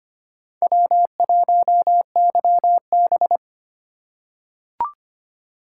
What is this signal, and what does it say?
Morse code, heard as a keyed tone, about 25 wpm, 700 Hz. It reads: W1YB
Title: USA Call Signs - 2x Character Spacing - 25wpm